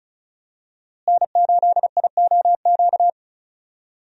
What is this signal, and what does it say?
Morse code, heard as a keyed tone, about 36 wpm, 700 Hz. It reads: N8IOQ
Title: USA Call Signs - 5 Characters - Repeated 2x - 35wpm